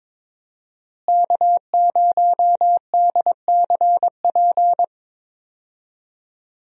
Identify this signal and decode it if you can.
Morse code, heard as a keyed tone, about 22 wpm, 700 Hz. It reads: K0DCP
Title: USA Call Signs - 5 Characters - Repeated 2x - 22wpm